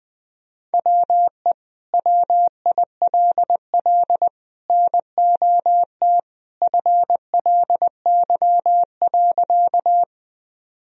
Morse code, heard as a keyed tone, about 21 wpm, 700 Hz. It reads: WE WILL NOT FLY.
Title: Sentences from Top 1000 Words - Review Part 2 - 20wpm